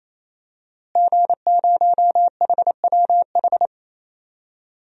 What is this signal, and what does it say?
Morse code, heard as a keyed tone, about 28 wpm, 700 Hz. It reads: G0HWH
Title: DXCC-100 Call Signs - 5 Characters - Repeated 3x - 28wpm